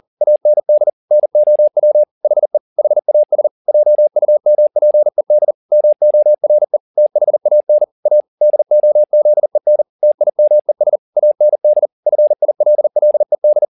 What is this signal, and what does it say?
Morse code, heard as a keyed tone, about 40 wpm, 600 Hz. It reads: AND NOW HE HAS JUMPED MORE THAN A DOZEN TIMES AND FILLED
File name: the old man and the sea 2 40wpm